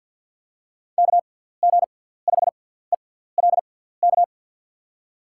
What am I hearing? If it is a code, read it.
Morse code, heard as a keyed tone, about 51 wpm, spaced as for 17 wpm, 700 Hz. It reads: KG5ELK
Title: USA Call Signs - 6x Character Spacing - 50wpm